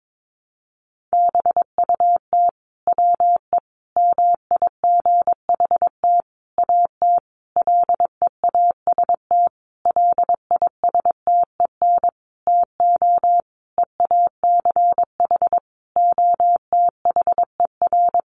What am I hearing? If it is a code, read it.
Morse code, heard as a keyed tone, about 22 wpm, 700 Hz. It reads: BUT WE MIGHT AT LEAST LISTEN TO EACH OTHER
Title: Sentences from Top 1300 Words 22wpm